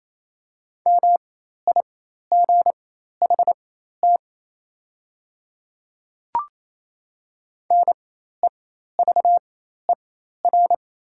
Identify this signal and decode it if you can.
Morse code, heard as a keyed tone, about 28 wpm, spaced as for 13 wpm, 700 Hz. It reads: MIGHT NEVER
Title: 4x Character Spacing - Top 300 Words - 28wpm